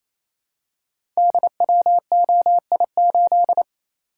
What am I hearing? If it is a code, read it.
Morse code, heard as a keyed tone, about 29 wpm, 700 Hz. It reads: DWOI8
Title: Code Groups - 5 Letter:Numbers - 28wpm